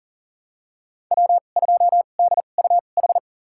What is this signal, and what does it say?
Morse code, heard as a keyed tone, about 40 wpm, spaced as for 29 wpm, 700 Hz. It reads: W2DUH
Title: USA Call Signs - 5 Characters - 2x Character Spacing - 40wpm